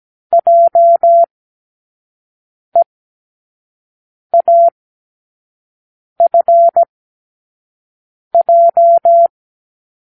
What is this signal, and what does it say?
Morse code, heard as a keyed tone, about 17 wpm, spaced as for 5 wpm, 680 Hz. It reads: JEAFJ
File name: k7qo / Lesson18(A-L)